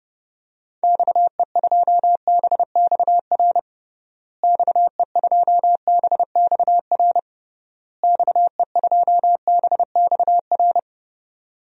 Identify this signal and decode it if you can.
Morse code, heard as a keyed tone, about 32 wpm, 700 Hz. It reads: XE2BXR XE2BXR XE2BXR
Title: DXCC-100 Call Signs - 6 Characters - Repeated 3x - 30wpm